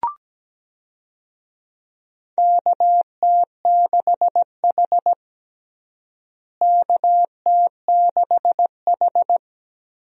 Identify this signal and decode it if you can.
Morse code, heard as a keyed tone, about 17 wpm, 700 Hz. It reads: KT6H KT6H
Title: USA Call Signs - 4 Characters - Repeated 3x - 17wpm